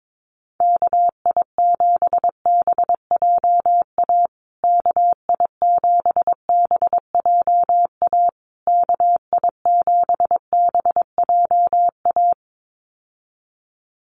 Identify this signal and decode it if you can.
Morse code, heard as a keyed tone, about 22 wpm, 700 Hz. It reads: KI7BJA KI7BJA KI7BJA
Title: USA Call Signs Repeated 3x 22wpm